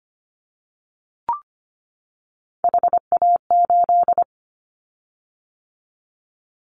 Morse code, heard as a keyed tone, about 25 wpm, 700 Hz. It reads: HA8
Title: Callsign Prefixes for Top 50 DXCC - 25wpm